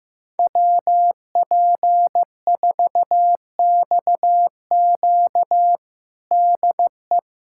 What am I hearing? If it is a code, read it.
Morse code, heard as a keyed tone, about 15 wpm, 700 Hz. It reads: WP4XQ DE
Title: USA Callsign DE Callsign 15wpm